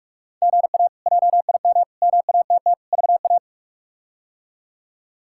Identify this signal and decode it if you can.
Morse code, heard as a keyed tone, about 46 wpm, 700 Hz. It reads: GA JIM MATT VA
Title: K1USN SST QSO Practice - 45wpm